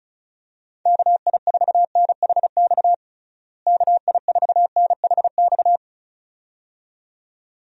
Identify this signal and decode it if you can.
Morse code, heard as a keyed tone, about 36 wpm, 700 Hz. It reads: KI4NHX KI4NHX
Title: USA Call Signs - 6 Characters - Repeated 2x - 35wpm